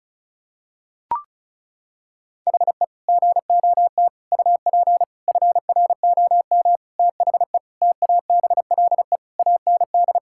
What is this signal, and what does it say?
Morse code, heard as a keyed tone, about 36 wpm, 700 Hz. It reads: HE GOT UP FROM THE TABLE AND
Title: Sentences from Top 1600 Words 35wpm